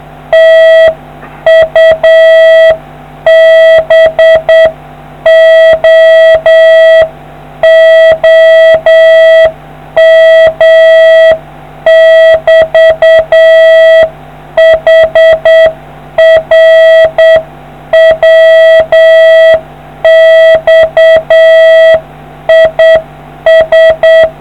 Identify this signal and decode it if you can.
Morse code, heard as a keyed tone, about 8 wpm, 650 Hz. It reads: TU BOOM <BT> HR WX IS